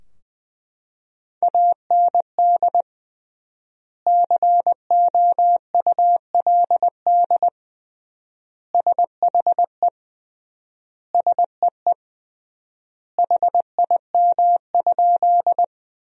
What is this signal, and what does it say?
Morse code, heard as a keyed tone, about 21 wpm, 700 Hz. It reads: AND COULD SHE SEE HIM?
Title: 3x Word-Spacing - Sentences from Top 100 Words - 20wpm